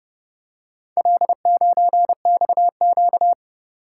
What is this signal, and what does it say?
Morse code, heard as a keyed tone, about 29 wpm, 700 Hz. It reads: L9XQ